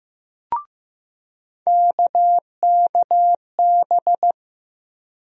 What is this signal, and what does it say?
Morse code, heard as a keyed tone, about 15 wpm, 700 Hz. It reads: KKB
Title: USA Call Sign Postfixes - 3 Letters 15wpm